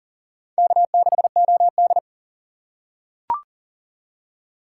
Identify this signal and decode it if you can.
Morse code, heard as a keyed tone, about 40 wpm, 700 Hz. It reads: K6OD
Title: USA Call Signs - 4 Characters - 4x Character Spacing - 40wpm